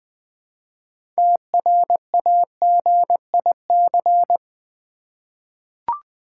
Morse code, heard as a keyed tone, about 21 wpm, 700 Hz. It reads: TRAGIC
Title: 6 Letter Words - First 3 Letters Given - 20wpm